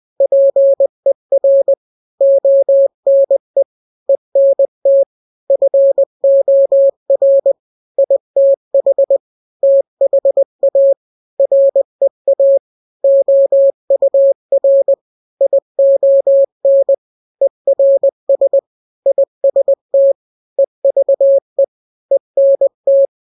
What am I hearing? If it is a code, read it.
Morse code, heard as a keyed tone, about 20 wpm, 550 Hz. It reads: PER ONE ENT FOR ITH THA REA OUR ION ERS IST EVE ENT
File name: n-grams / 3-grams 20 18 WPM